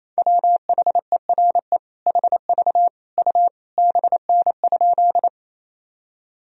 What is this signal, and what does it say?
Morse code, heard as a keyed tone, about 28 wpm, 700 Hz. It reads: WHERE HV U BN?
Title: Sentences from Top 200 Words and Abbreviations 28wpm